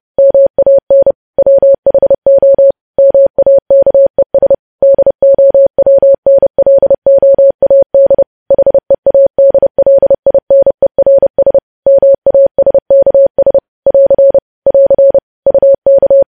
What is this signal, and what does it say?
Morse code, heard as a keyed tone, about 30 wpm, 550 Hz. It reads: MAN WHO MAKES DOWNLOAD HEADLINERS MASKS <AR> <AR> UK
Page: News Headlines in Morse Code at 30 WPM